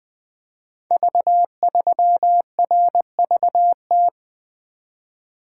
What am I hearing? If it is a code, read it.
Morse code, heard as a keyed tone, about 20 wpm, 700 Hz. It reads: V3RVT
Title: Code Groups - 5 Letter:Numbers - 20wpm